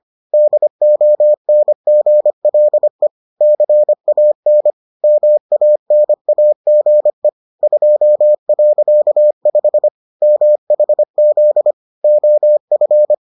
Morse code, heard as a keyed tone, about 25 wpm, 600 Hz. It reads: DONGLE CAN MANAGE 2.5 MHZ OF